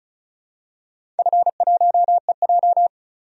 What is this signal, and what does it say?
Morse code, heard as a keyed tone, about 36 wpm, 700 Hz. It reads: F1EJ
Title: DXCC-100 Call Signs - 4 Characters - 35wpm